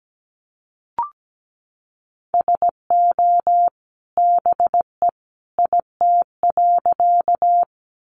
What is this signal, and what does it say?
Morse code, heard as a keyed tone, about 17 wpm, 700 Hz. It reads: SO BE IT.
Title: Sentences from Top 100 Words 17wpm